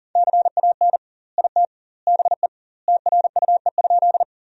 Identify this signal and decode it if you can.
Morse code, heard as a keyed tone, about 37 wpm, 700 Hz. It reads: CAN IT BE TRUE?
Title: 2x Word-Spacing - Sentences from Top 400 Words - 40wpm